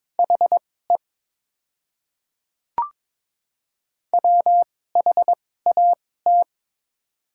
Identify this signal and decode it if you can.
Morse code, heard as a keyed tone, about 22 wpm, spaced as for 16 wpm, 690 Hz. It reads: HE WHAT
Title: 2x Character Spacing - Top 100 Words - 22wpm